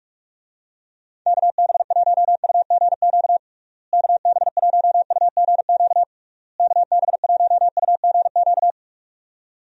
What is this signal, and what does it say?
Morse code, heard as a keyed tone, about 46 wpm, 700 Hz. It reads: KB1UGQ KB1UGQ KB1UGQ
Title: USA Call Signs - 6 Characters - Repeated 3x - 45wpm